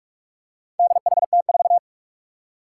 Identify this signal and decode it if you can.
Morse code, heard as a keyed tone, about 46 wpm, 700 Hz. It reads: DHT4
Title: Four Challenging Letter-Number Combinations 45wpm